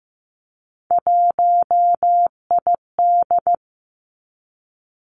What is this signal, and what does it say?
Morse code, heard as a keyed tone, about 15 wpm, 700 Hz. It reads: 1ID
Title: Three Challenging Letter-Number Combinations 15wpm